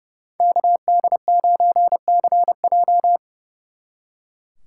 Morse code, heard as a keyed tone, about 37 wpm, 700 Hz. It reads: KD9CJ
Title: Speed Racing - USA Call Signs 30wpm